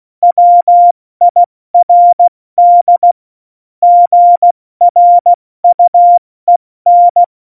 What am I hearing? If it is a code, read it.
Morse code, heard as a keyed tone, about 16 wpm, spaced as for 14 wpm, 700 Hz. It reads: WIRD GRUEN
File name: Morsekurs / Witze / Witz 04 August 2020, 70 F80